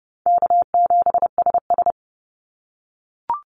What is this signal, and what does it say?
Morse code, heard as a keyed tone, about 30 wpm, 700 Hz. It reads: K7SS
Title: Easy Call Signs - 30wpm